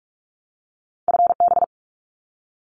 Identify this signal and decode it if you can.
Morse code, heard as a keyed tone, about 46 wpm, 700 Hz. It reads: FB
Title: Abbreviations 45wpm